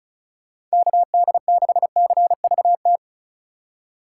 Morse code, heard as a keyed tone, about 36 wpm, 700 Hz. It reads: KD6CVT
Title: USA Call Signs - 6 Characters - 4x Character Spacing - 35wpm